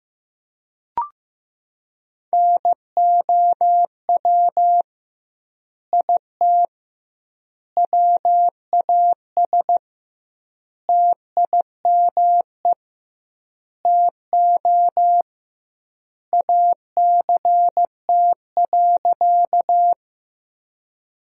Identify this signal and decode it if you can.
Morse code, heard as a keyed tone, about 15 wpm, 700 Hz. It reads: NOW IT WAS TIME TO ACT.
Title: 2x Word-Spacing - Sentences from Top 200 Words - 15wpm